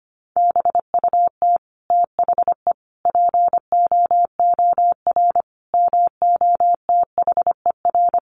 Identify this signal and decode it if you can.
Morse code, heard as a keyed tone, about 25 wpm, 700 Hz. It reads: BUT THE POOR MOTHER
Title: Sentences from Top 1000 Words 25wpm